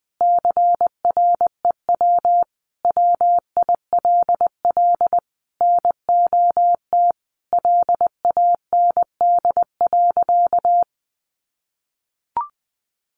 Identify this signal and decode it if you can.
Morse code, heard as a keyed tone, about 19 wpm, 700 Hz. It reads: CREW WILL NOT LAND.
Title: Sentences from Top 1500 Words 20wpm